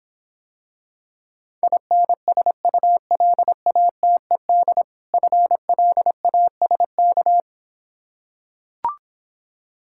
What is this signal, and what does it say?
Morse code, heard as a keyed tone, about 26 wpm, 700 Hz. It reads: INSULATED FLASK